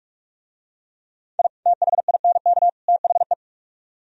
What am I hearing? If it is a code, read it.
Morse code, heard as a keyed tone, about 40 wpm, 700 Hz. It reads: I THINK THE